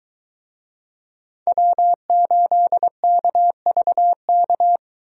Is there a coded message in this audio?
Yes: Morse code from 1.5 to 4.8 s, heard as a keyed tone, about 22 wpm, 700 Hz. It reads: W8KVK